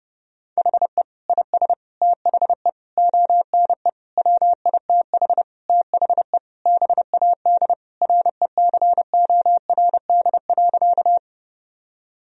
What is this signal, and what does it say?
Morse code, heard as a keyed tone, about 32 wpm, 700 Hz. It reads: HE IS THE ONE WITH THE BAD RECORD.
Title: Sentences from Top 1000 Words - Review Part 3 - 30wpm